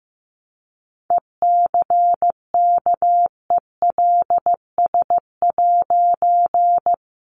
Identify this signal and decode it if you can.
Morse code, heard as a keyed tone, about 15 wpm, 700 Hz. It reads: ECKELS'
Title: A Sound Of Thunder 15wpm